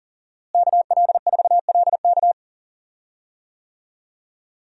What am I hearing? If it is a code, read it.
Morse code, heard as a keyed tone, about 37 wpm, 700 Hz. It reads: KL4LK
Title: USA Call Signs - 5 Characters - Repeated 3x - 40wpm